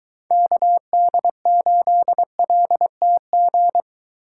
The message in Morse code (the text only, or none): KD8LTG